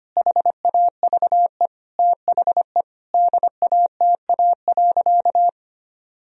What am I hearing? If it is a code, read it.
Morse code, heard as a keyed tone, about 25 wpm, 700 Hz. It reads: HAVE THE DATA.
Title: Sentences from Top 1000 Words - Review Part 3 - 25wpm